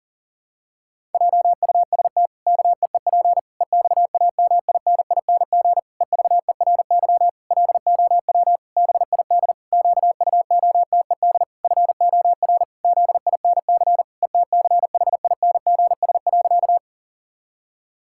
Morse code, heard as a keyed tone, about 40 wpm, 700 Hz. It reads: JUST KEEP EXAMINING EVERY LOW BID QUOTED FOR ZINC ETCHINGS.
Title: Pangram Challenge v2 - 40wpm